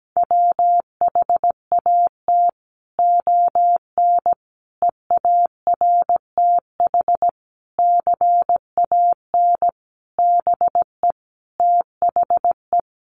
Morse code, heard as a keyed tone, about 17 wpm, 700 Hz. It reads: WHAT ON EARTH CAN BE THE
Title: Sentences from Top 700 Words 17wpm